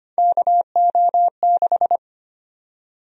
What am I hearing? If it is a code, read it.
Morse code, heard as a keyed tone, about 25 wpm, 700 Hz. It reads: KO6
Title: USA Call Sign Prefixes - 25wpm